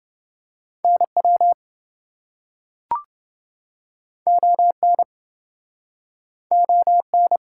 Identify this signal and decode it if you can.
Morse code, heard as a keyed tone, about 28 wpm, 700 Hz. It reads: NW ON ON